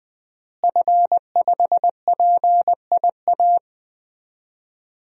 Morse code, heard as a keyed tone, about 20 wpm, 700 Hz. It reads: F5PIA
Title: DXCC-100 Call Signs - 5 Characters - Repeated 3x - 20wpm